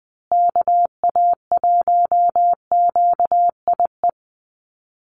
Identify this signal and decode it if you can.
Morse code, heard as a keyed tone, about 20 wpm, 700 Hz. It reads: KA1QIE
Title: USA Call Signs Repeated 3x 20wpm